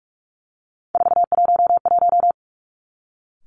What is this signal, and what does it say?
Morse code, heard as a keyed tone, about 46 wpm, 700 Hz. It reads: 411
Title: Three Challenging Letter-Number Combinations 45wpm